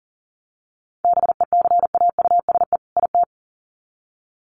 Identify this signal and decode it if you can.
Morse code, heard as a keyed tone, about 40 wpm, 700 Hz. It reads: BECAUSE IT
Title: Sets of 2 Words 40wpm